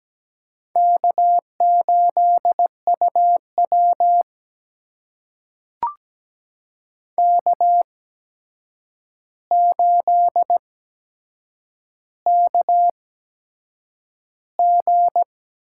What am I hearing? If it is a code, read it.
Morse code, heard as a keyed tone, about 17 wpm, spaced as for 7 wpm, 700 Hz. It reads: K8UW K8KG
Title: USA Call Signs - 4 Characters - 8x Character Spacing - 17wpm